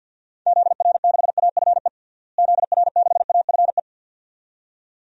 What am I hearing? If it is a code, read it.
Morse code, heard as a keyed tone, about 51 wpm, 700 Hz. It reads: ZR6AFE ZR6AFE
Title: DXCC-100 Call Signs - 6 Characters - Repeated 2x - 50wpm